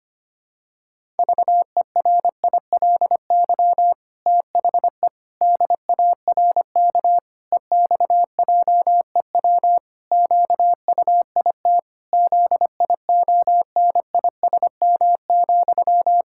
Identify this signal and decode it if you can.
Morse code, heard as a keyed tone, about 25 wpm, 700 Hz. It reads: VERILY THE DARK EXJEW QUIT ZIONISM,
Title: Pangram Challenge v2 - 25wpm